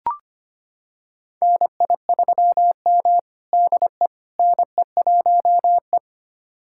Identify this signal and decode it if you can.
Morse code, heard as a keyed tone, about 25 wpm, 700 Hz. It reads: NI3M DE NE1E
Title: USA Callsign DE Callsign - 4 Characters - 25wpm